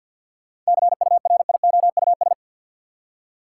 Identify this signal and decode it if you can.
Morse code, heard as a keyed tone, about 52 wpm, 700 Hz. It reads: CURIOUS
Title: Seven Letter Words - 50wpm